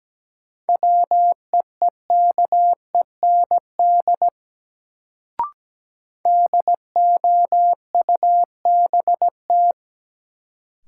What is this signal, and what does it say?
Morse code, heard as a keyed tone, about 17 wpm, 700 Hz. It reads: WEEKEND DOUBT
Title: Sets of 5 Words - New Word Drill - 17wpm